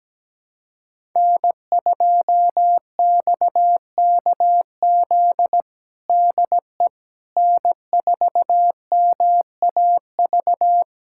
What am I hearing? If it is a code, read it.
Morse code, heard as a keyed tone, about 17 wpm, 700 Hz. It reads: N2XKZ DE N4MAV
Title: USA Callsign DE Callsign - 5 Characters - 17wpm